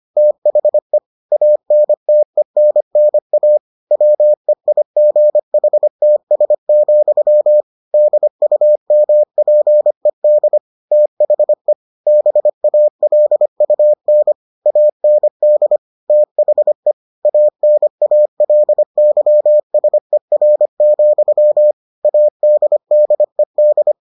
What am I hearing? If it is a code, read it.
Morse code, heard as a keyed tone, about 25 wpm, 600 Hz. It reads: THE ANTENNA WEIGHTS, DUMPED THE BALUN AND THE ANALYSER, ADDED